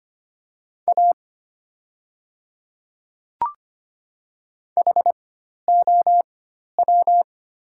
Morse code, heard as a keyed tone, about 25 wpm, spaced as for 12 wpm, 700 Hz. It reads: A HOW
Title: 4x Character Spacing - Top 200 Words - 25wpm